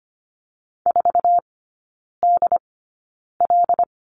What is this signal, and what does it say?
Morse code, heard as a keyed tone, about 25 wpm, spaced as for 9 wpm, 690 Hz. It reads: 4DL
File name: Lesson 15e - Number 5 - Three Characters 25wpm